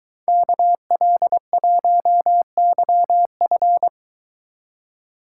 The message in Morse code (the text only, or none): KL1YF